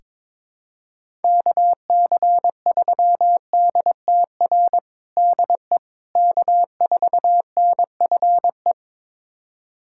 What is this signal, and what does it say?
Morse code, heard as a keyed tone, about 22 wpm, 700 Hz. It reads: KC3DTR DE K4NFE
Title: USA Callsign DE Callsign 22wpm